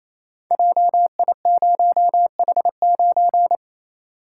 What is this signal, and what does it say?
Morse code, heard as a keyed tone, about 27 wpm, 700 Hz. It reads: JI0H9
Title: Code Groups - 5 Letter:Numbers - 28wpm